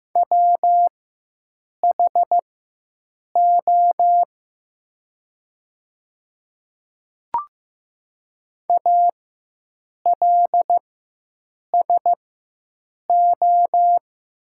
Morse code, heard as a keyed tone, about 15 wpm, spaced as for 7 wpm, 700 Hz. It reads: WHO ALSO
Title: 4x Character Spacing - Top 300 Words - 15wpm